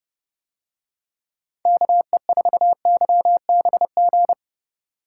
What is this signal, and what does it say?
Morse code, heard as a keyed tone, about 30 wpm, 700 Hz. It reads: KE4YBG